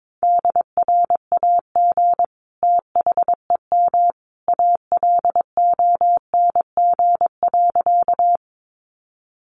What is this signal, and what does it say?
Morse code, heard as a keyed tone, about 22 wpm, 700 Hz. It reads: DRAG THEM ALONG.
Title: Sentences from Top 1500 Words 22wpm